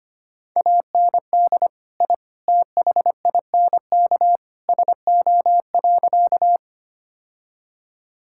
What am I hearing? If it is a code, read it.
Morse code, heard as a keyed tone, about 25 wpm, 700 Hz. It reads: AND I THINK SO.
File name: practice / 25wpm / Sentences with 3-Word Phrases - 25wpm